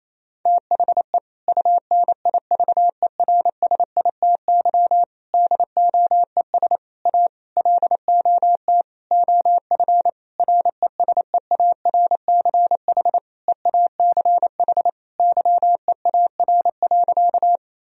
Morse code, heard as a keyed tone, about 28 wpm, 700 Hz. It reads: THE UNIVERSITY DOES A LOT OF RESEARCH EACH YEAR.
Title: Sentences from Top 1000 Words - Review Part 4 - 28wpm